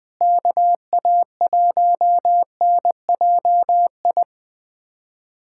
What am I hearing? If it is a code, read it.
Morse code, heard as a keyed tone, about 21 wpm, 700 Hz. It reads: KA1NJI
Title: USA Callsign DE Callsign - 6 Characters - 20wpm